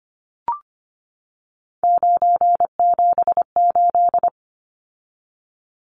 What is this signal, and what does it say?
Morse code, heard as a keyed tone, about 25 wpm, 700 Hz. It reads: 978
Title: Numbers - 3-Digit 25wpm